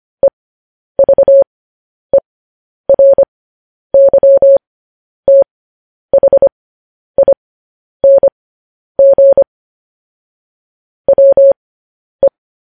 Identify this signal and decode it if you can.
Morse code, heard as a keyed tone, about 25 wpm, spaced as for 10 wpm, 550 Hz. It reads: EVERYTHING WE